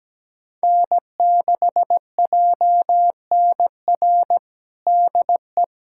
Morse code, heard as a keyed tone, about 17 wpm, 700 Hz. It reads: N6JNR DE
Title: USA Callsign DE Callsign 17wpm